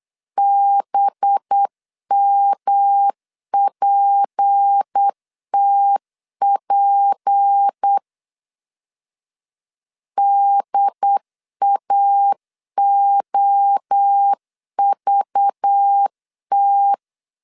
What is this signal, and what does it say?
Morse code, heard as a keyed tone, about 8 wpm, 800 Hz. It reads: BMPTP DAOVT
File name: CW / DET / 1 6 1500C S C 8wpm 5219